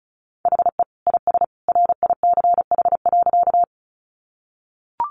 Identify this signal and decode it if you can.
Morse code, heard as a keyed tone, about 34 wpm, 700 Hz. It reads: HE IS RICH.
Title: Sentences from Top 1100 Words 35wpm